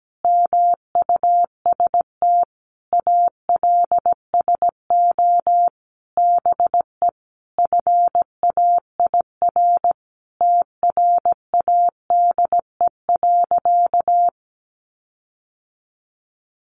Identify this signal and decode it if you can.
Morse code, heard as a keyed tone, about 17 wpm, 700 Hz. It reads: MUST ALSO BE FAIR TRADE.
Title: Sentences from Top 1300 Words 17wpm